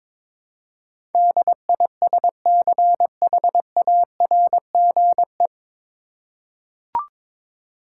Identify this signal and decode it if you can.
Morse code, heard as a keyed tone, about 22 wpm, 700 Hz. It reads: DISCHARGE
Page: Morse Code Ninja practice 22wpm